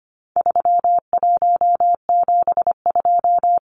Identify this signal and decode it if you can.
Morse code, heard as a keyed tone, about 25 wpm, 700 Hz. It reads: 3172
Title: Numbers - 4-Digit 25wpm